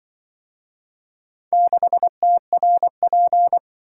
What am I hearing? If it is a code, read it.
Morse code, heard as a keyed tone, about 24 wpm, 700 Hz. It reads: 6TRP
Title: Speed-Racing - Four Letter-Number Combinations - 20wpm